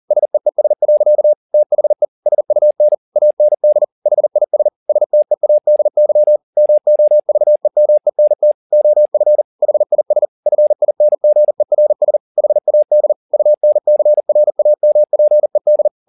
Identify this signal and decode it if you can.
Morse code, heard as a keyed tone, about 40 wpm, 600 Hz. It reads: SEES. THE SUN AND HIS STEADY MOVEMENT OF HIS FINGERS HAD UNCRAMPED